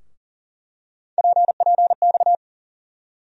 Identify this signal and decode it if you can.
Morse code, heard as a keyed tone, about 40 wpm, 700 Hz. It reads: PPX
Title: USA Call Sign Postfixes - 3 Letters 40wpm